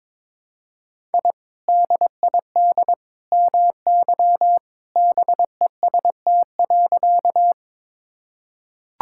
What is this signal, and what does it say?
Morse code, heard as a keyed tone, about 22 wpm, 700 Hz. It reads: I DID MY BEST.
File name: practice / 22wpm / Sentences with 2-Word Phrases - 22wpm